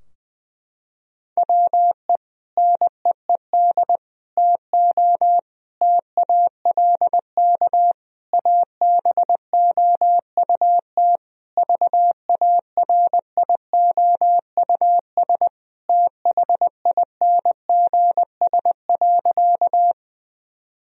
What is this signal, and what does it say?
Morse code, heard as a keyed tone, about 20 wpm, 700 Hz. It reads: WE NEED TO TALK ABOUT VARIOUS THINGS.
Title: Sentences from Top 1000 Words - Review Part 4 - 20wpm